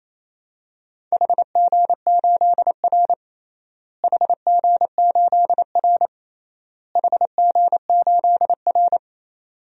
Morse code, heard as a keyed tone, about 28 wpm, 700 Hz. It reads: HG8R HG8R HG8R
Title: DXCC-100 Call Signs - 4 Characters - Repeated 3x - 28wpm